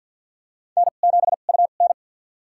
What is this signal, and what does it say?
Morse code, heard as a keyed tone, about 52 wpm, spaced as for 36 wpm, 700 Hz. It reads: N7UN
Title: USA Call Signs - 4 Characters - 2x Character Spacing - 50wpm